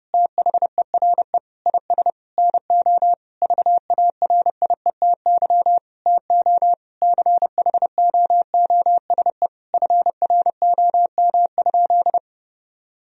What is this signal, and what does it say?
Morse code, heard as a keyed tone, about 32 wpm, 700 Hz. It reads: THERE IS NO VARIETY TO CHOOSE FROM?
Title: Sentences from Top 1000 Words - Review Part 4 - 30wpm